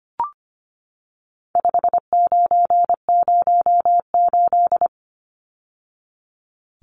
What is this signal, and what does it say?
Morse code, heard as a keyed tone, about 25 wpm, 700 Hz. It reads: 5908
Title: Numbers - 4-Digit 25wpm